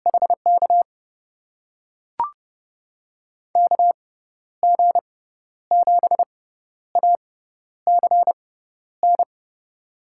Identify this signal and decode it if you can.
Morse code, heard as a keyed tone, about 30 wpm, spaced as for 12 wpm, 700 Hz. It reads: HK KG7ACN
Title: USA Call Signs - 6 Characters - 6x Character Spacing - 30wpm